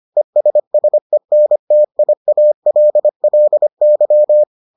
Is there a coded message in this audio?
Yes: Morse code from 0.2 to 4.4 s, heard as a keyed tone, about 25 wpm, 600 Hz. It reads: ESSENTIALLY